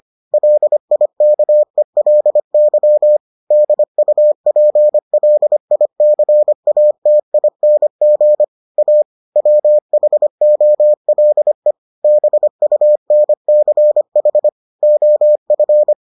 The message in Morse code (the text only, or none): LIKELY DUPLICATING A WHOLE BUNCH OF